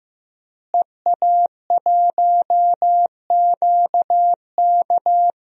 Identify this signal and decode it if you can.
Morse code, heard as a keyed tone, about 15 wpm, 700 Hz. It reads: EA1QK
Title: DXCC-100 Call Signs - 5 Characters - Repeated 3x - 15wpm